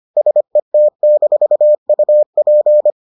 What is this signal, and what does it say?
Morse code, heard as a keyed tone, about 25 wpm, 600 Hz. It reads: SET-UP